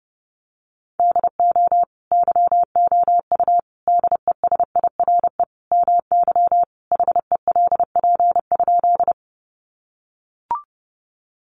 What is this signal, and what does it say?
Morse code, heard as a keyed tone, about 30 wpm, 700 Hz. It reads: DO YOU DESIRE MY HELP?
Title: Sentences from Top 1100 Words 30wpm